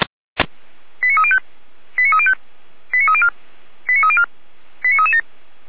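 Five-tone selective call (ZVEI), heard as quick runs of five tones, tones of about 80 ms. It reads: 89305 89305 89304 89304 89337
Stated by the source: ZVEI five-tone call received by SDR, from the server of feuerwehr-orlamuende.org (fire brigade Orlamünde).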